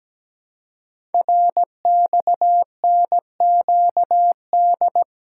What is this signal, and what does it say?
Morse code, heard as a keyed tone, about 17 wpm, 700 Hz. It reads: RXNQD
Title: Code Groups - 5 Letters - 17wpm